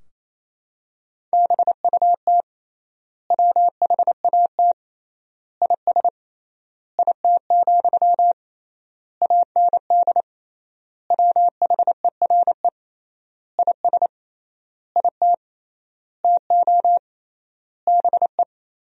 Morse code, heard as a keyed tone, about 28 wpm, 700 Hz. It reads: BUT WHAT IS IT, AND WHERE IS IT TO BE
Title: 3x Word-Spacing - Sentences from Top 300 Words - 28wpm